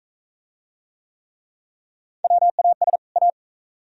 Morse code, heard as a keyed tone, about 44 wpm, 700 Hz. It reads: WAS A